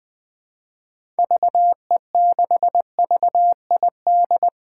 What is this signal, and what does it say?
Morse code, heard as a keyed tone, about 20 wpm, 700 Hz. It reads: VE6VID
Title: 2023 SOTA Chaser Call Signs - 20wpm